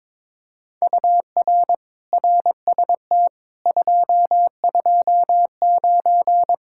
Morse code, heard as a keyed tone, about 22 wpm, 700 Hz. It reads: UR RST 229
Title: Speed-Racing Common QSO Phrases - 22wpm